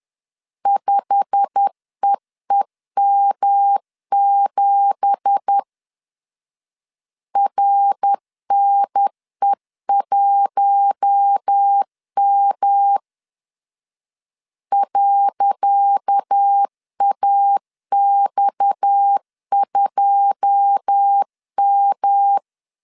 Morse code, heard as a keyed tone, about 10 wpm, 800 Hz. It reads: 5EEM7 RNE1M .AX2M
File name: LEZIONI / 10 SIGNS / 10C DOT 10wpm 2940